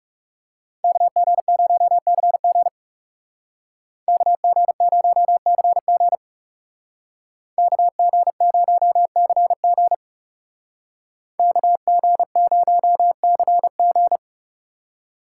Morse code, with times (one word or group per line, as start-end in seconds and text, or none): 0.8-2.7 KG0CG
4.1-6.2 KG0CG
7.6-9.9 KG0CG
11.4-14.2 KG0CG